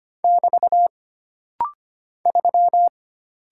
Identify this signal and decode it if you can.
Morse code, heard as a keyed tone, about 25 wpm, 700 Hz. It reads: <BT> 3
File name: Single Character (Letter:Number:Punctuation:Prosign) - Rapid-Fire - 25wpm